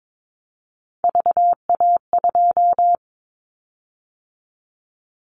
Morse code, heard as a keyed tone, about 22 wpm, 700 Hz. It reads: VA2
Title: Callsign Prefixes for Top 100 DXCC - 22wpm